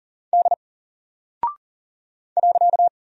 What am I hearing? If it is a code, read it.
Morse code, heard as a keyed tone, about 40 wpm, 700 Hz. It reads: D .
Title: Single Character (Letter:Number:Punctuation:Prosign) Alt 40wpm